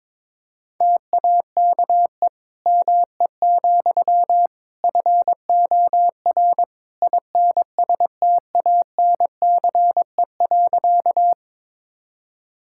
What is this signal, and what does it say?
Morse code, heard as a keyed tone, about 22 wpm, 700 Hz. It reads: TAKE ME, FOR INSTANCE.
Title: Sentences from Top 1000 Words - Review Part 5 - 22wpm